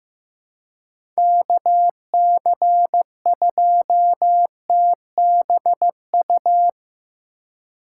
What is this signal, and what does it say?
Morse code, heard as a keyed tone, about 15 wpm, 700 Hz. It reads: KC2TBU
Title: USA Call Signs - 6 Characters - 4x Character Spacing - 15wpm